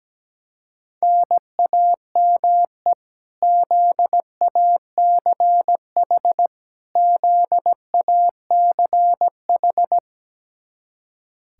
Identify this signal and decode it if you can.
Morse code, heard as a keyed tone, about 17 wpm, 700 Hz. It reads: NAME ZACH ZACH
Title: Practice QSO - Exchange 1 - Beginning - 17wpm